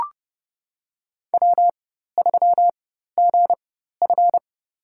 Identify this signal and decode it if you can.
Morse code, heard as a keyed tone, about 32 wpm, spaced as for 14 wpm, 700 Hz. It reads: W3GF
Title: USA Call Signs - 4 Characters - 4x Character Spacing - 30wpm